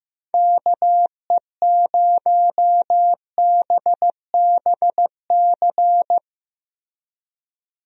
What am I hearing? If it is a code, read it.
Morse code, heard as a keyed tone, about 15 wpm, 700 Hz. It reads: KE0BBC
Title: USA Call Signs - 6 Characters - Repeated 2x - 15wpm